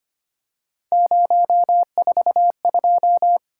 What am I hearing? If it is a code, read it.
Morse code, heard as a keyed tone, about 25 wpm, 700 Hz. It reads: 042
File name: Three Letter-Number Combinations - 25wpm